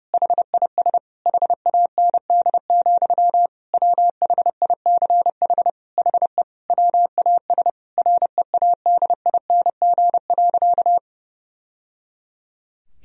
Morse code, heard as a keyed tone, about 32 wpm, 700 Hz. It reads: HIS HAND, WHICH HE WAS READING.
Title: Sentences from Top 1000 Words - Review Part 3 - 30wpm